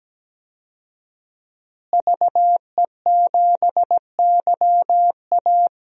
Morse code, heard as a keyed tone, about 17 wpm, 700 Hz. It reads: VE7YA
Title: DXCC-100 Call Signs - 5 Characters - Repeated 2x - 17wpm